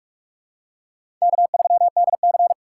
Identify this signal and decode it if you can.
Morse code, heard as a keyed tone, about 44 wpm, 690 Hz. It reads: K3DC
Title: USA Call Signs - 4x Character Spacing - 45wpm